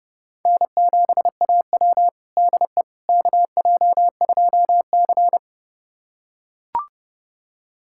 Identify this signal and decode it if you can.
Morse code, heard as a keyed tone, about 32 wpm, 700 Hz. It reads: N7AW DE KJ2C
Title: USA Callsign DE Callsign - 4 Characters - 30wpm